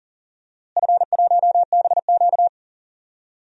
Morse code, heard as a keyed tone, about 40 wpm, 700 Hz. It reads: F1BQ
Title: DXCC-100 Call Signs - 4 Characters - Repeated 2x - 40wpm